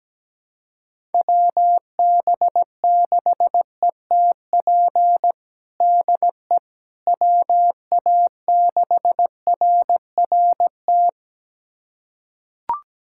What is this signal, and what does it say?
Morse code, heard as a keyed tone, about 17 wpm, 700 Hz. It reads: WB6ETP DE WA6RRT
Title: USA Callsign DE Callsign - 6 Characters - 17wpm